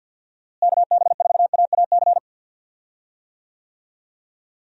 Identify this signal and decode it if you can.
Morse code, heard as a keyed tone, about 52 wpm, 700 Hz. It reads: KB4AAC
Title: USA Call Signs - 6 Characters - Repeated 2x - 50wpm